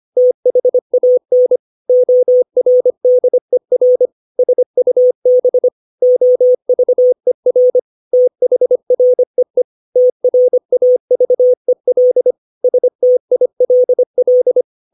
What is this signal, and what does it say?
Morse code, heard as a keyed tone, about 25 wpm, 500 Hz. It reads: THAN ORDER SUB OVER THREE TRAVEL STILL
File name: 632 words / 632 Words 25WPM